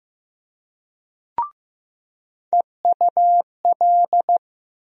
Morse code, heard as a keyed tone, about 15 wpm, 700 Hz. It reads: EUL
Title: USA Call Sign Postfixes - 3 Letters 15wpm